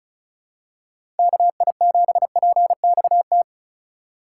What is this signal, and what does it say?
Morse code, heard as a keyed tone, about 36 wpm, 700 Hz. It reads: KI7PXT
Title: USA Call Signs - 6 Characters - 4x Character Spacing - 35wpm